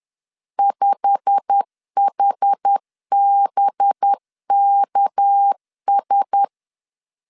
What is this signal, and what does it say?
Morse code, heard as a keyed tone, about 10 wpm, 800 Hz. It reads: 5HBKS
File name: LEZIONI / 8 YJ45 / 8 YJ45 10wpm 2549